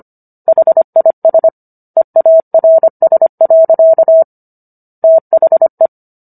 Morse code, heard as a keyed tone, about 25 wpm, 650 Hz. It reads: HIS EARS. THE